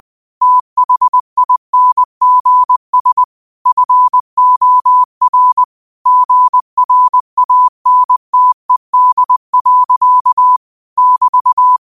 Morse code, heard as a keyed tone, about 20 wpm, 1000 Hz. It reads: THINGS FOR GRANTED. <BT>
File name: qotdfiles / 20wpm